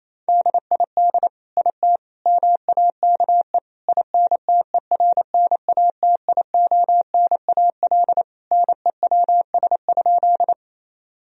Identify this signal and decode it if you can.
Morse code, heard as a keyed tone, about 28 wpm, 700 Hz. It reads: DID IT MAKE INTERNATIONAL NEWS?
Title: Sentences from Top 1000 Words - Review Part 4 - 28wpm